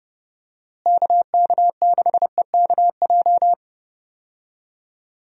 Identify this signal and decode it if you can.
Morse code, heard as a keyed tone, about 31 wpm, 700 Hz. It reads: KK6EKJ
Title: USA Call Signs - 6 Characters - Repeated 3x - 30wpm